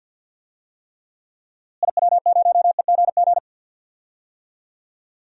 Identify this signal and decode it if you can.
Morse code, heard as a keyed tone, about 52 wpm, 700 Hz. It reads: IW0EGG
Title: DXCC-100 Call Signs - 6 Characters - Repeated 2x - 50wpm